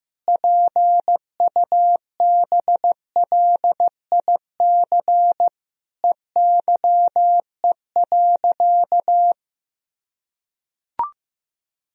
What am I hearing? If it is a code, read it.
Morse code, heard as a keyed tone, about 16 wpm, 700 Hz. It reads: PUBLIC EYE.
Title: Sentences from Top 1000 Words - Review Part 3 - 15wpm